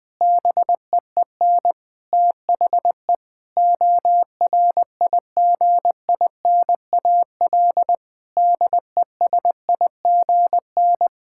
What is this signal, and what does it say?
Morse code, heard as a keyed tone, about 20 wpm, 700 Hz. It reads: BEEN THE ORIGINAL DESIGN
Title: Sentences from Top 1000 Words - Review Part 6 - 20wpm